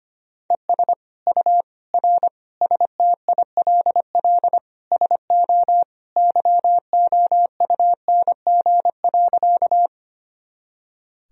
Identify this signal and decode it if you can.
Morse code, heard as a keyed tone, about 25 wpm, 700 Hz. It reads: ES U R STILL SO YOUNG.
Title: Sentences from Top 400 Words and Abbreviations 25wpm